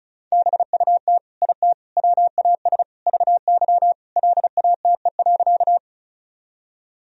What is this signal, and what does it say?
Morse code, heard as a keyed tone, about 34 wpm, 700 Hz. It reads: BUT IT WAS VY LATE.
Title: Sentences from Top 300 Words and Abbreviations 35wpm